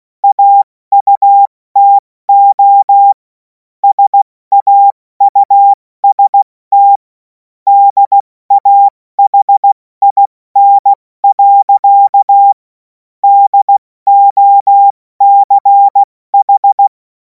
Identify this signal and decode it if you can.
Morse code, heard as a keyed tone, about 16 wpm, 800 Hz. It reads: AUTO SAUST DAHIN. DOCH
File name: Morsekurs / Witze / Witz 28. April 2026 70 F80